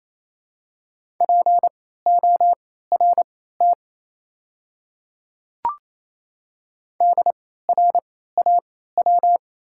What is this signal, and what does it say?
Morse code, heard as a keyed tone, about 28 wpm, spaced as for 16 wpm, 700 Hz. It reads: PORT DRAW
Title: 3x Character Spacing - Top 300 Words - 28wpm